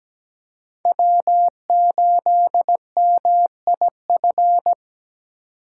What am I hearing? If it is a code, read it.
Morse code, heard as a keyed tone, about 20 wpm, 700 Hz. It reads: W8MIF
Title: Speed Racing - USA 1x3 Call Signs 17wpm